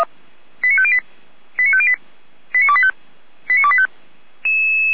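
Five-tone selective call (ZVEI), heard as quick runs of five tones, tones of about 70 ms. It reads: 89508 89508 89285 89285
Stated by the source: ZVEI five-tone selective call 89150, received by SDR.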